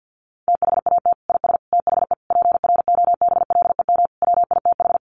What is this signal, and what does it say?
Morse code, heard as a keyed tone, about 50 wpm, 700 Hz. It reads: THAT IS THE PROBLEM WITH
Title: Sentences from Top 1100 Words 50wpm